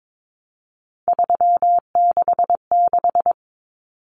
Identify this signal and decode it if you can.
Morse code, heard as a keyed tone, about 22 wpm, 700 Hz. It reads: 366
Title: Common Cut Numbers - 3-Digit 22wpm